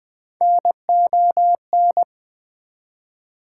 Morse code, heard as a keyed tone, about 21 wpm, 700 Hz. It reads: NON
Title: Prefixes and Suffixes 20wpm